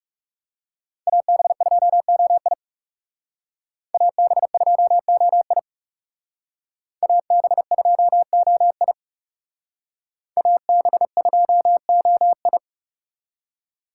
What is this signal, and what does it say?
Morse code, heard as a keyed tone, about 36 wpm, 700 Hz. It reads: AB2OI AB2OI AB2OI AB2OI
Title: Speed Racing - USA 2x2 Call Signs 30wpm